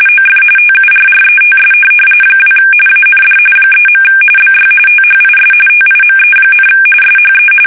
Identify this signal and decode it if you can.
RTTY, about 45.45 baud. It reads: OF TURRETS 2 AND 3. BY THIS TIME, SHE
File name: fsk850 / USS-Massachusetts